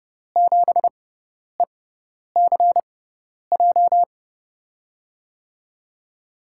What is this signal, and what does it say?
Morse code, heard as a keyed tone, about 30 wpm, spaced as for 10 wpm, 700 Hz. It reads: 7ECJ
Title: USA Call Signs - 6 Characters - 6x Character Spacing - 30wpm